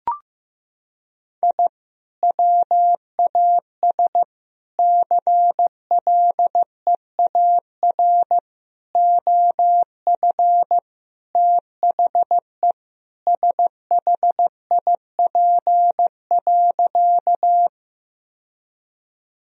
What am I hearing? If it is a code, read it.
Morse code, heard as a keyed tone, about 15 wpm, 700 Hz. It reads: I WAS CLEAR OF THE SHIP.
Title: Sentences from Top 1000 Words - Review Part 2 - 15wpm